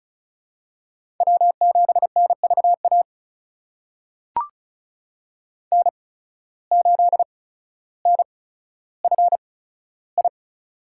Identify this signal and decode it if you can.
Morse code, heard as a keyed tone, about 36 wpm, spaced as for 15 wpm, 700 Hz. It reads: W7NVA N8NFI
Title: USA Call Signs - 5 Characters - 8x Character Spacing - 35wpm